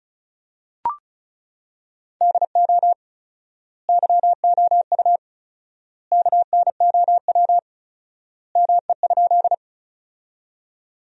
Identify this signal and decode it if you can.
Morse code, heard as a keyed tone, about 36 wpm, 700 Hz. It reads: DO YOU KNOW ME?
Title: 4x Word-Spacing - Sentences from Top 200 Words - 35wpm